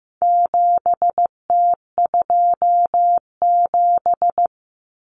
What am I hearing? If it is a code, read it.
Morse code, heard as a keyed tone, about 15 wpm, 700 Hz. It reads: 7T27
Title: Common Cut Numbers - 4-Digit 15wpm